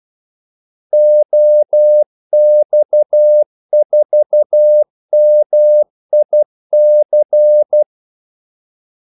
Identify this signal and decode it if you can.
Morse code, heard as a keyed tone, about 12 wpm, 600 Hz. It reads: OX4MIC
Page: 2 ekstra ordmellemrum mellem kaldesignalerne